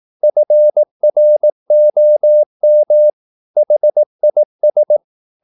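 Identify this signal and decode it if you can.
Morse code, heard as a keyed tone, about 18 wpm, 600 Hz. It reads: FROM HIS
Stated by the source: Morse practice at 18 wpm, English text, from CW med Gnister.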